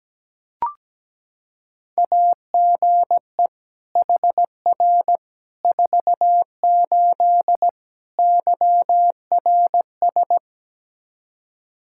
Morse code, heard as a keyed tone, about 17 wpm, 700 Hz. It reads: AGE HR 48 YRS
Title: Practice QSO - Exchange 3 - More Info - 17wpm